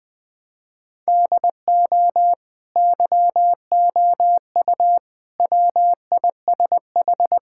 Morse code, heard as a keyed tone, about 20 wpm, 700 Hz. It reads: DO YOU WISH
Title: Sentences from Top 1600 Words 20wpm